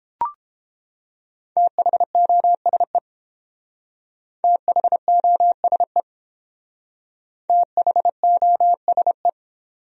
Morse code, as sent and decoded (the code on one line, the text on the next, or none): - .... --- ... / - .... --- ... / - .... --- ... .
THOSE THOSE THOSE